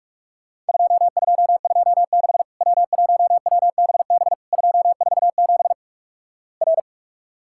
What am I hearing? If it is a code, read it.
Morse code, heard as a keyed tone, about 47 wpm, 700 Hz. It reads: 2226 W1WBB 247 R
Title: Worked All Europe DX Contest - QTC Exchange - 45wpm